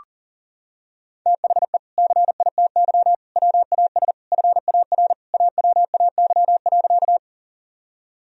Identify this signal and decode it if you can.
Morse code, heard as a keyed tone, about 40 wpm, 700 Hz. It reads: THE CITY WAS FAR AWAY.
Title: Sentences from Top 1000 Words - Review Part 1 - 40wpm